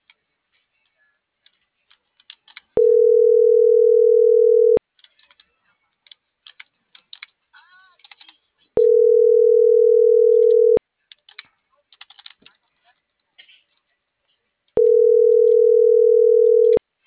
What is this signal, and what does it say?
Ringback tone (North America), 440 + 480 Hz.